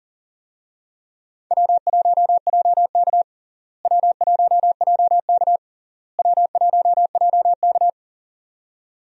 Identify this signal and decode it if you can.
Morse code, heard as a keyed tone, about 40 wpm, 700 Hz. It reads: W1JK W1JK W1JK
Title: USA Call Signs - 4 Characters - Repeated 3x - 40wpm